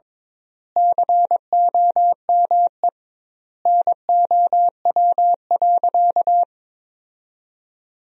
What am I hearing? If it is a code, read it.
Morse code, heard as a keyed tone, about 22 wpm, 700 Hz. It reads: COME NOW.
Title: 2x Word-Spacing - Sentences from Top 500 Words - 22wpm